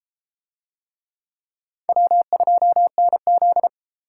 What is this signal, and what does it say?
Morse code, heard as a keyed tone, about 30 wpm, 700 Hz. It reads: W2NZ